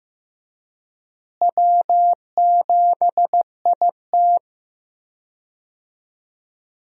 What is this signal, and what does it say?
Morse code, heard as a keyed tone, about 15 wpm, 700 Hz. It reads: W7IT
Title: USA Call Signs - 4 Characters - Repeated 3x - 15wpm